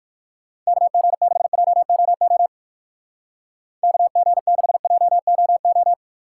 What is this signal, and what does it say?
Morse code, heard as a keyed tone, about 59 wpm, 700 Hz. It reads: KG6JOO KG6JOO